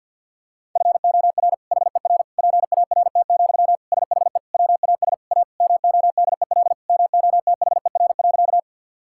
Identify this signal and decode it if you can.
Morse code, heard as a keyed tone, about 50 wpm, 700 Hz. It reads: FOR HER PART, SHE WAS A MODEL MOTHER.
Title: Sentences from Top 1000 Words - Review Part 5 - 50wpm